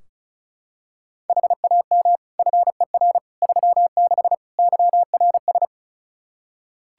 Morse code, heard as a keyed tone, about 34 wpm, 700 Hz. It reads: HAM FER 36 YRS
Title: Practice QSO - Exchange 3 - More Info - 35wpm